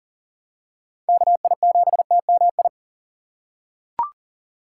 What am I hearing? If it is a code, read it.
Morse code, heard as a keyed tone, about 40 wpm, 690 Hz. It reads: KI7TMI
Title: USA Call Signs - 4x Character Spacing - 40wpm